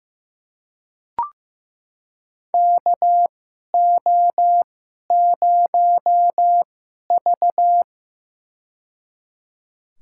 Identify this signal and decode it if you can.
Morse code, heard as a keyed tone, about 15 wpm, spaced as for 11 wpm, 700 Hz. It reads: KO0V
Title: USA Call Signs - 4 Characters - 2x Character Spacing - 15wpm